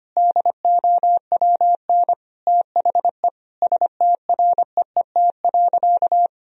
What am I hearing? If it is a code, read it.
Morse code, heard as a keyed tone, about 25 wpm, 700 Hz. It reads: DOWN THE STREET.
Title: Sentences from Top 1000 Words - Review Part 3 - 25wpm